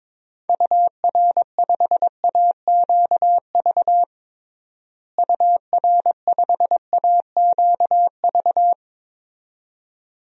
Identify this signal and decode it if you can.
Morse code, heard as a keyed tone, about 21 wpm, 700 Hz. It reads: UR5AQV UR5AQV
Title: DXCC-100 Call Signs - 6 Characters - Repeated 2x - 22wpm